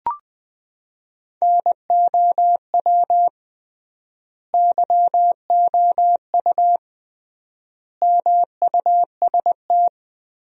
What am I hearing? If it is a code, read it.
Morse code, heard as a keyed tone, about 21 wpm, 700 Hz. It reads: NOW YOU MUST
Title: 3x Word-Spacing - Sentences from Top 300 Words - 20wpm